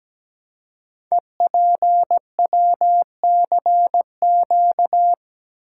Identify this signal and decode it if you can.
Morse code, heard as a keyed tone, about 17 wpm, 700 Hz. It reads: EPWCQ
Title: Code Groups - 5 Letter:Numbers - 17wpm